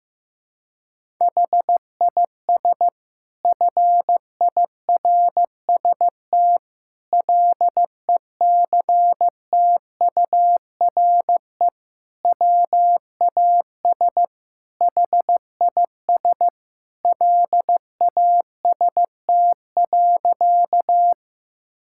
Morse code, heard as a keyed tone, about 15 wpm, 700 Hz. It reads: HIS FIRST LECTURE WAS HIS LAST.
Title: Sentences from Top 1600 Words 15wpm